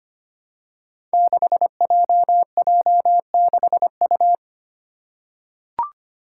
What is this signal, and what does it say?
Morse code, heard as a keyed tone, about 25 wpm, 700 Hz. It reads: 6JJ6U
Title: Five Challenging Letter-Number Combinations 25wpm